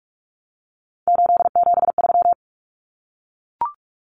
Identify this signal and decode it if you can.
Morse code, heard as a keyed tone, about 44 wpm, 700 Hz. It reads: 873
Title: Common Cut Numbers - 3-Digit 45wpm